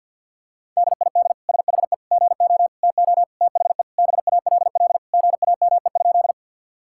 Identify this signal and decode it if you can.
Morse code, heard as a keyed tone, about 50 wpm, 700 Hz. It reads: DID SHE GO TO THE BALL GAME?
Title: Sentences from Top 1000 Words - Review Part 6 - 50wpm